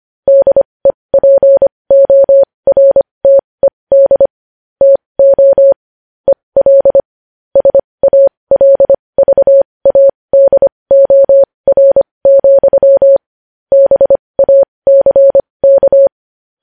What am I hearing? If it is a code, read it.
Morse code, heard as a keyed tone, about 25 wpm, spaced as for 20 wpm, 550 Hz. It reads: DEPORTED TO EL SALVADOR, BACK
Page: News Headlines in Morse Code at 20 WPM